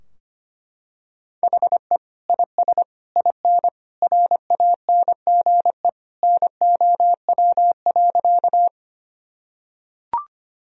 Morse code, heard as a keyed tone, about 25 wpm, 700 Hz. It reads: HE IS IN RANGE NOW.
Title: Sentences from Top 1000 Words - Review Part 4 - 25wpm